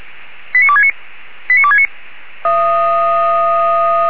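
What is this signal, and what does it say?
Five-tone selective call (ZVEI), heard as quick runs of five tones, tones of about 70 ms. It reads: 89269 89269
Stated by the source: ZVEI five-tone selective call 89273 received by SDR.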